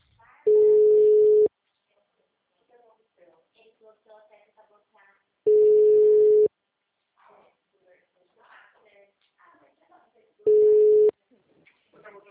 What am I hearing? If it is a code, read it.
Ringback tone (Europe, 425 Hz), with speech.